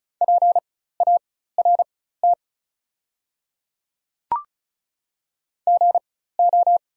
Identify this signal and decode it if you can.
Morse code, heard as a keyed tone, about 36 wpm, spaced as for 16 wpm, 700 Hz. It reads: PART GO
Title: 4x Character Spacing - Top 500 Words - 35wpm